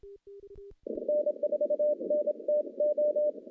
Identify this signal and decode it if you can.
Morse code, heard as a keyed tone, about 27 wpm, 600 Hz. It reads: N4NTO
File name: contests / CW-SS-k5tr- -0000